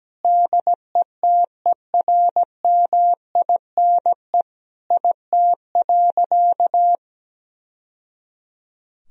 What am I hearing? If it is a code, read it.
Morse code, heard as a keyed tone, about 17 wpm, 700 Hz. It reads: DETERMINE IT.
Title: Sentences from Top 1000 Words - Review Part 4 - 17wpm